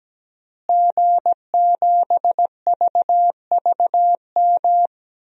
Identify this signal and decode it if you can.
Morse code, heard as a keyed tone, about 17 wpm, 700 Hz. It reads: G7VVM
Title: DXCC-100 Call Signs - 5 Characters - Repeated 3x - 17wpm